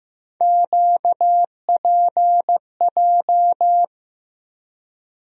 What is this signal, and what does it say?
Morse code, heard as a keyed tone, about 15 wpm, 700 Hz. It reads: QPJ
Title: USA Call Sign Postfixes - 3 Letters 15wpm